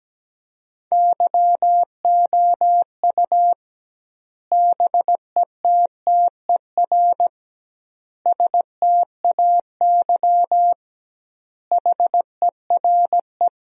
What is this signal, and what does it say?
Morse code, heard as a keyed tone, about 17 wpm, 700 Hz. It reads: YOU BETTER STAY HERE
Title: 2x Word-Spacing - Sentences from Top 500 Words - 17wpm